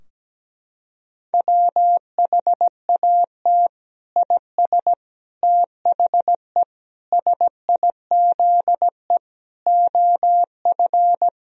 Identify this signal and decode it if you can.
Morse code, heard as a keyed tone, about 17 wpm, 700 Hz. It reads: WHAT IS THE SIZE OF
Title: Sentences from Top 1000 Words - Review Part 2 - 17wpm